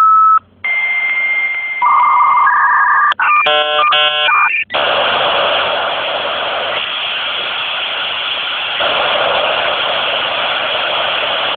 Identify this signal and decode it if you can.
Dial-up modem handshake.
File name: 56K Modem Mono Ringtone